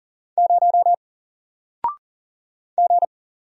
Morse code, heard as a keyed tone, about 40 wpm, 700 Hz. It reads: 0 G
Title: Single Letter-Number Alt 40wpm